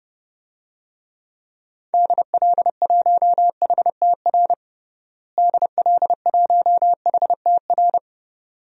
Morse code, heard as a keyed tone, about 30 wpm, 700 Hz. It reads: DL1HTR DL1HTR
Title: DXCC-100 Call Signs - 6 Characters - Repeated 2x - 30wpm